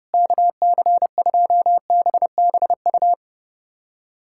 Morse code, heard as a keyed tone, about 30 wpm, 700 Hz. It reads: KC2BBU
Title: K1USN SST QSO Practice - 30wpm